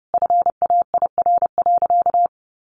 Morse code, heard as a keyed tone, about 30 wpm, 700 Hz. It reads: FAIR.
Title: Sentences from Top 1300 Words 30wpm